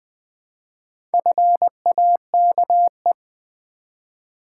Morse code, heard as a keyed tone, about 20 wpm, 700 Hz. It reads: FAKE
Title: Build A Word - 1 Letter At A Time - 4-Letter Words - 20wpm